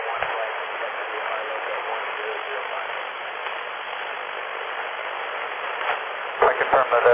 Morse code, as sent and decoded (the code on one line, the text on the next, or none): none